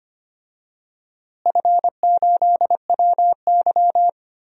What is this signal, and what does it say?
Morse code, heard as a keyed tone, about 25 wpm, 700 Hz. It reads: F8WY
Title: DXCC-100 Call Signs - 4 Characters - Repeated 3x - 25wpm